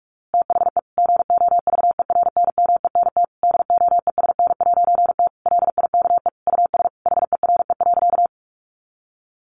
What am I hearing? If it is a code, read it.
Morse code, heard as a keyed tone, about 44 wpm, 700 Hz. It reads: THE GOVERNMENT DOESN'T LIKE US HERE.
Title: A Sound Of Thunder 45wpm